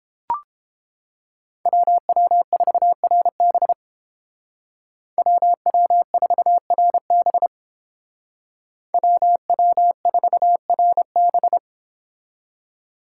Morse code, heard as a keyed tone, about 30 wpm, 700 Hz. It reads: WW4RB WW4RB WW4RB